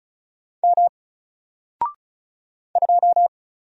Morse code, heard as a keyed tone, about 34 wpm, 700 Hz. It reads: M 2
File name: Single-Letter-Number-35wpm